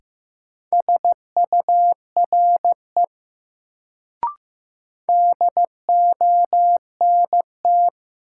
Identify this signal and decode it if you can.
Morse code, heard as a keyed tone, about 15 wpm, 700 Hz. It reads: SURE DONT
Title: Sentences from Top 300 Words - New Word Drill - 15wpm